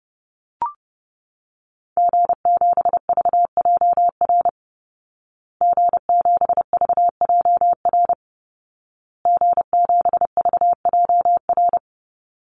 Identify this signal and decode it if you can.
Morse code, heard as a keyed tone, about 30 wpm, 700 Hz. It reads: G7VJR G7VJR G7VJR
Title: Call Signs Repeated 2x - 30wpm